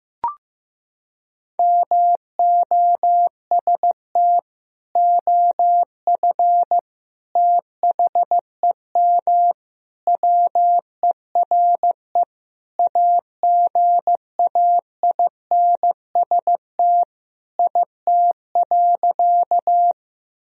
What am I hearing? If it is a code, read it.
Morse code, heard as a keyed tone, about 15 wpm, 700 Hz. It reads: MOST OF THEM WERE AGAINST IT.
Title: Sentences from Top 1000 Words - Review Part 1 - 15wpm